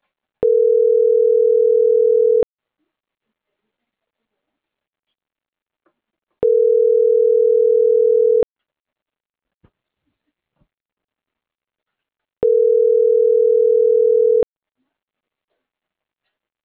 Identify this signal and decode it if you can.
Ringback tone (North America), 440 + 480 Hz.